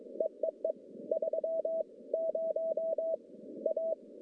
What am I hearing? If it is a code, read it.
Morse code, heard as a keyed tone, about 22 wpm, 640 Hz. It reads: EEE 3 0 A
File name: contests / CW-SS-k5tr- -1100